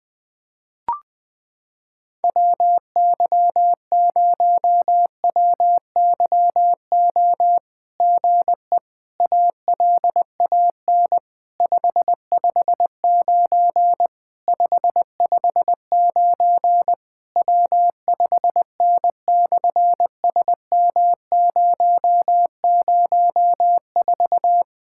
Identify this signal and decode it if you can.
Morse code, heard as a keyed tone, about 20 wpm, 700 Hz. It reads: WY0WYO GE ALAN 559 559 W5N/SM004
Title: SOTA QSO Practice 20wpm